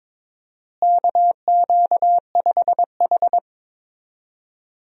Morse code, heard as a keyed tone, about 22 wpm, 700 Hz. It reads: KQ5H
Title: USA Call Signs - 4 Characters - Repeated 3x - 22wpm